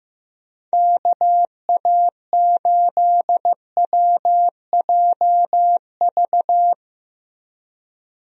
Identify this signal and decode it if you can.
Morse code, heard as a keyed tone, about 15 wpm, 700 Hz. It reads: KA8WJV
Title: USA Call Signs - 6 Characters - Repeated 2x - 15wpm